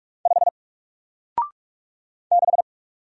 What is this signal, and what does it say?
Morse code, heard as a keyed tone, about 46 wpm, 700 Hz. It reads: 5 6
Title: Single Letter-Number 45wpm